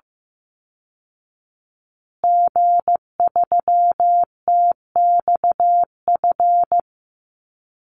Morse code, heard as a keyed tone, about 15 wpm, 700 Hz. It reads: G3TXF
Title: Call Signs Repeated 3x - 15wpm